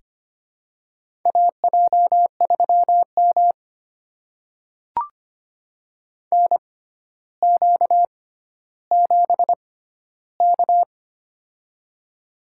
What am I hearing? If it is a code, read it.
Morse code, heard as a keyed tone, about 25 wpm, spaced as for 13 wpm, 700 Hz. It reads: AJ3M NQ7K
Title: USA Call Signs - 4 Characters - 6x Character Spacing - 25wpm